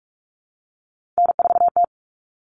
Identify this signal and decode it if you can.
Morse code, heard as a keyed tone, about 44 wpm, 700 Hz. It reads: N4T
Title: All Cut Numbers - 3-Digit 45wpm